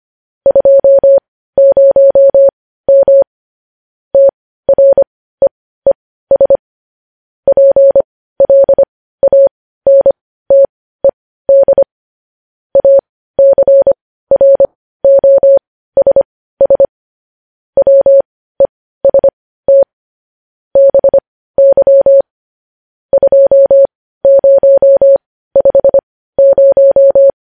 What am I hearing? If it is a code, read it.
Morse code, heard as a keyed tone, about 25 wpm, spaced as for 15 wpm, 550 Hz. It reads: 20M TREES PLANTED ACROSS WEST BY 2050
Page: news headlines in morse code at 15 wpm